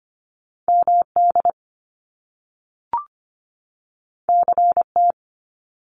Morse code, heard as a keyed tone, about 25 wpm, 700 Hz. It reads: MD CT
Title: US States 25wpm